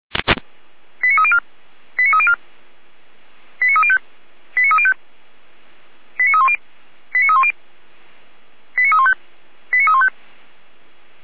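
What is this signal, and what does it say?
Five-tone selective call (ZVEI), heard as quick runs of five tones, tones of about 70 ms. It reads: 89304 89304 89305 89305 89310 89310 89315 89315
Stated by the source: ZVEI five-tone call received by SDR.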